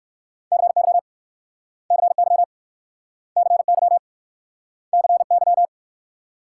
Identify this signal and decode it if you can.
Morse code, heard as a keyed tone, about 61 wpm, 690 Hz. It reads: CY CY CY CY